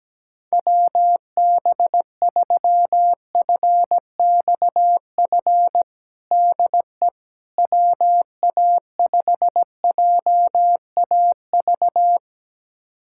Morse code, heard as a keyed tone, about 17 wpm, 700 Hz. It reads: WB3FXF DE WA5JAV
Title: USA Callsign DE Callsign 17wpm